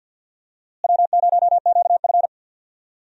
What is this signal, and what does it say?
Morse code, heard as a keyed tone, about 51 wpm, 700 Hz. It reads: W0QF
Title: USA Call Signs - 4 Characters - Repeated 2x - 50wpm